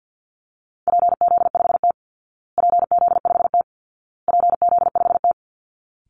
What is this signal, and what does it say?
Morse code, heard as a keyed tone, about 48 wpm, 700 Hz. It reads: PZ5T PZ5T PZ5T
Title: Call Signs Repeated 2x - 50wpm